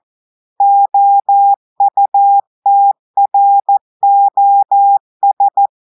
Morse code, heard as a keyed tone, about 14 wpm, 800 Hz. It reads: OUTROS
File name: audios / portugais / portugais texte en clair 2 vitesse 14 mots minutes